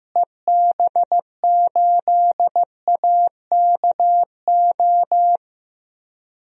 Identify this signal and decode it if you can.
Morse code, heard as a keyed tone, about 14 wpm, 700 Hz. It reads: EB8AKO
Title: DXCC-100 Call Signs - 6 Characters - Repeated 3x - 15wpm